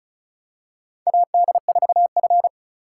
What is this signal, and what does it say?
Morse code, heard as a keyed tone, about 36 wpm, 700 Hz. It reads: AD4F
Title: USA Call Signs - 4 Characters - 4x Character Spacing - 35wpm